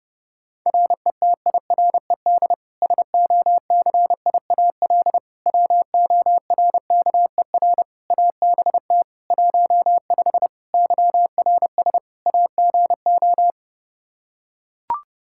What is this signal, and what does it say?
Morse code, heard as a keyed tone, about 32 wpm, 700 Hz. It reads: RETIRED SOCIAL WORKER ABT 15 YRS AGO
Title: Practice QSO - Exchange 3 - More Info - 30wpm